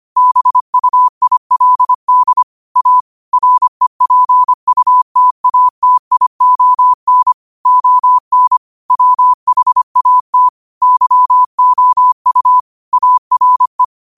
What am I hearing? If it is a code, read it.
Morse code, heard as a keyed tone, about 25 wpm, 1000 Hz. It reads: BUILD A REPUTATION ON WHAT YOU ARE
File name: qotdfiles / 25wpm